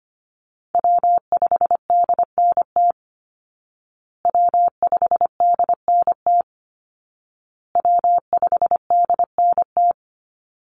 Morse code, heard as a keyed tone, about 25 wpm, 700 Hz. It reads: W5DNT W5DNT W5DNT
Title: Call Signs Repeated 2x - 25wpm